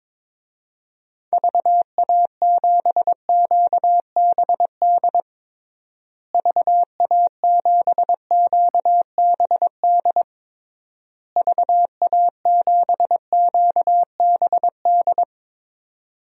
Morse code, heard as a keyed tone, about 22 wpm, 700 Hz. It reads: VA7QBD VA7QBD VA7QBD
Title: DXCC-100 Call Signs - 6 Characters - Repeated 3x - 22wpm